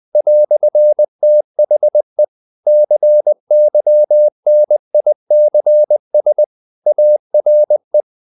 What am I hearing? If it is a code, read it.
Morse code, heard as a keyed tone, about 20 wpm, 600 Hz. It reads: "THE CYNICS ARE
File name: CW / Practice / EnglishQuotations0320